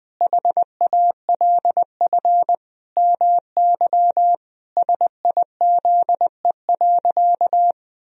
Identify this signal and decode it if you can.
Morse code, heard as a keyed tone, about 20 wpm, 700 Hz. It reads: HALF MY SIZE.
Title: Sentences from Top 1000 Words - Review Part 2 - 20wpm